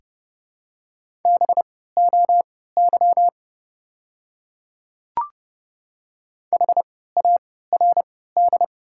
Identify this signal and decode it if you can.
Morse code, heard as a keyed tone, about 30 wpm, spaced as for 17 wpm, 700 Hz. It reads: BOY HARD
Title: 3x Character Spacing - Top 300 Words - 30wpm